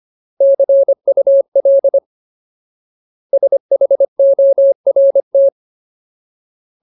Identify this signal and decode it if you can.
Morse code, heard as a keyed tone, about 25 wpm, 550 Hz. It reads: CUL SHORT
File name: Ultimate Word List / 25WPM Ultimate